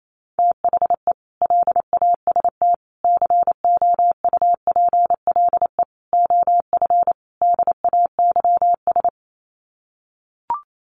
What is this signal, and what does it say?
Morse code, heard as a keyed tone, about 28 wpm, 700 Hz. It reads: THE LAST COUPLE OF DAYS
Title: Sets of 5 Words Encore 28wpm